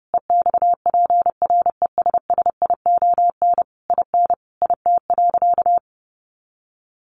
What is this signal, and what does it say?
Morse code, heard as a keyed tone, about 30 wpm, 700 Hz. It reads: EXPRESSION IN IT.
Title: Sentences from Top 1200 Words 30wpm